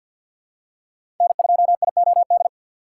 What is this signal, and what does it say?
Morse code, heard as a keyed tone, about 46 wpm, 690 Hz. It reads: N2IOD
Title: USA Call Signs - 2x Character Spacing - 50wpm